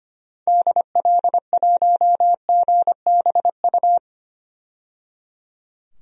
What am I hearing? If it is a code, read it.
Morse code, heard as a keyed tone, about 25 wpm, 700 Hz. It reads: DL1GBU
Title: DXCC-100 Call Signs - 6 Characters - 25wpm